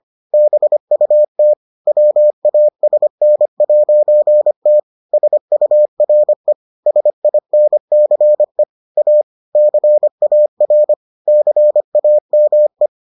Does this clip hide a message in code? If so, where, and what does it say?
Yes: Morse code from 0.3 to 12.9 s, heard as a keyed tone, about 25 wpm, 600 Hz. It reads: BUT WASN'T SURE SINCE A CAR CAME